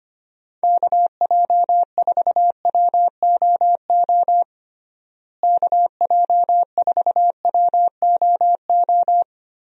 Morse code, heard as a keyed tone, about 25 wpm, 700 Hz. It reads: KJ4WOO KJ4WOO
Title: USA Call Signs - 6 Characters - Repeated 3x - 25wpm